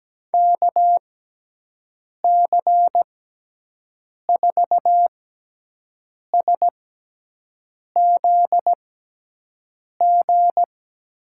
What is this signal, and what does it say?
Morse code, heard as a keyed tone, about 17 wpm, spaced as for 6 wpm, 700 Hz. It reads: KC4SZG
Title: USA Call Signs - 6 Characters - 6x Character Spacing - 17wpm